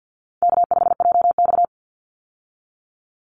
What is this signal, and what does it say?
Morse code, heard as a keyed tone, about 50 wpm, 700 Hz. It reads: K5JX
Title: Easy Call Signs - 50wpm